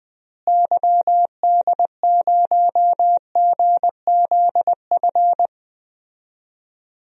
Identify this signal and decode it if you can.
Morse code, heard as a keyed tone, about 20 wpm, 700 Hz. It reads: YD0GZF
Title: DXCC-100 Call Signs - 6 Characters - Repeated 3x - 20wpm